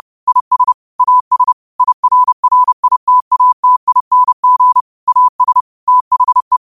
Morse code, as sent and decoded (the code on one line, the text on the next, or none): .. ... / .- ... / .. .-. .-. .. - .- - .. -. --. / .- ... / - .... .
IS AS IRRITATING AS THE